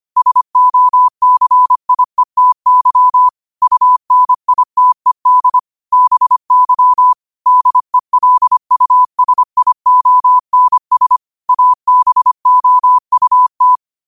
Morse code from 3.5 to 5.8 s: UNITED

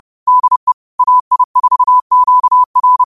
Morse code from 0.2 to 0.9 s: DE

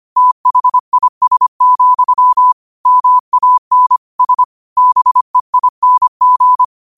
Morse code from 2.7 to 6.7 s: MAN S BEING

